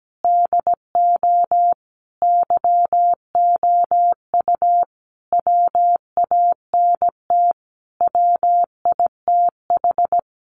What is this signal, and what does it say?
Morse code, heard as a keyed tone, about 17 wpm, 700 Hz. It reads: DO YOU WANT WITH
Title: Sentences from Top 200 Words 17wpm